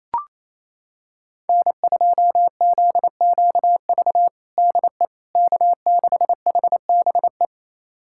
Morse code, heard as a keyed tone, about 27 wpm, 700 Hz. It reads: N2ZQV DE K6HBE
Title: USA Callsign DE Callsign 28wpm